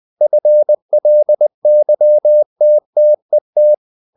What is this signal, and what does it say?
Morse code, heard as a keyed tone, about 20 wpm, 600 Hz. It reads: FLYTTET
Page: Ferie 20 wpm